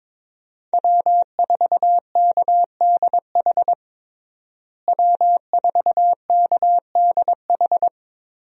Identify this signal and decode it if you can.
Morse code, heard as a keyed tone, about 21 wpm, 700 Hz. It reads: W4KDH W4KDH
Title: USA Call Signs - 5 Characters - Repeated 2x - 22wpm